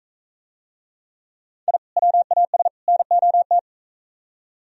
Morse code, heard as a keyed tone, about 41 wpm, 700 Hz. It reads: I WAS NOT